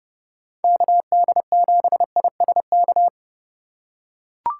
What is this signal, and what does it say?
Morse code, heard as a keyed tone, about 28 wpm, 700 Hz. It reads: KD7ISK